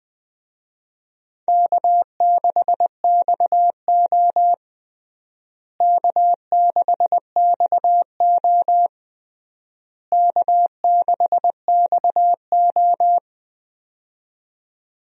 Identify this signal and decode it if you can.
Morse code, heard as a keyed tone, about 20 wpm, 700 Hz. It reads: K6XO K6XO K6XO
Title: USA Call Signs - 4 Characters - Repeated 3x - 20wpm